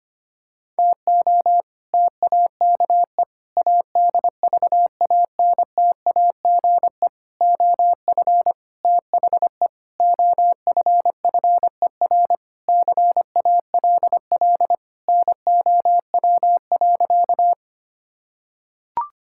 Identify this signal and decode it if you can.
Morse code, heard as a keyed tone, about 25 wpm, 700 Hz. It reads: TO TAKE ADVANTAGE OF THE OFFER CALL NOW.
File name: practice / 25wpm / Sentences with 4-Word Phrases - 25wpm